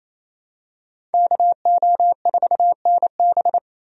Morse code, heard as a keyed tone, about 27 wpm, 700 Hz. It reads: KO4NB
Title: USA Call Signs - 5 Characters - Repeated 3x - 28wpm